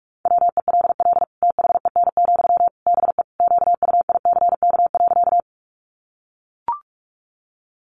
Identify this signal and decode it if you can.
Morse code, heard as a keyed tone, about 44 wpm, 700 Hz. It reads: WELL THEN, BE QUICK.
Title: Sentences from Top 500 Words 45wpm